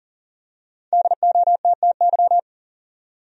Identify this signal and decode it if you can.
Morse code, heard as a keyed tone, about 40 wpm, 700 Hz. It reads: DOTTY
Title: USA Common Names - Max 5 Letters - 40wpm